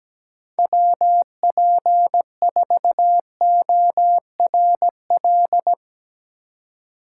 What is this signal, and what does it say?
Morse code, heard as a keyed tone, about 18 wpm, 700 Hz. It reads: WP4ORL
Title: Speed Racing - USA 2x3 Call Signs 17wpm